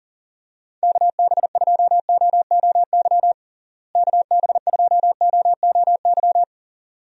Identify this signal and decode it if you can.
Morse code, heard as a keyed tone, about 37 wpm, 700 Hz. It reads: KB2OOY KB2OOY
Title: USA Call Signs - 6 Characters - Repeated 2x - 40wpm